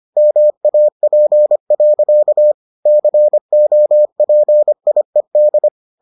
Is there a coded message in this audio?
Yes: Morse code from 0.2 to 5.7 s, heard as a keyed tone, about 25 wpm, 600 Hz. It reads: MAP. COPIED